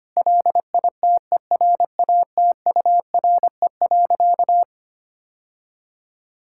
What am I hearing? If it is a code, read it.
Morse code, heard as a keyed tone, about 25 wpm, 700 Hz. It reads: LITERATURE.
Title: Sentences from Top 1000 Words - Review Part 6 - 25wpm